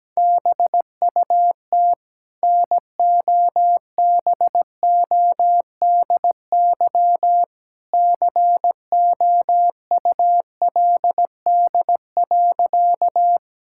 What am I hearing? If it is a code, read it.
Morse code, heard as a keyed tone, about 17 wpm, 700 Hz. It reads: BUT NOBODY COULD.
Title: Sentences from Top 1600 Words 17wpm